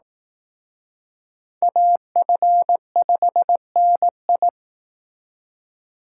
Morse code, heard as a keyed tone, about 19 wpm, 700 Hz. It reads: AF5NI
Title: Speed Racing - USA 2x2 Call Signs 15wpm